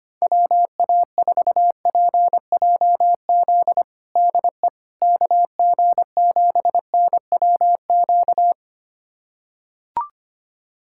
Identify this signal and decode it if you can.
Morse code, heard as a keyed tone, about 25 wpm, 700 Hz. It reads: WA4PJZ DE KG7NWQ
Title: USA Callsign DE Callsign 25wpm